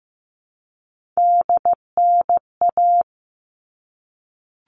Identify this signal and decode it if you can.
Morse code, heard as a keyed tone, about 15 wpm, 700 Hz. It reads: DNA
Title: All Cut Numbers - 3-Digit 15wpm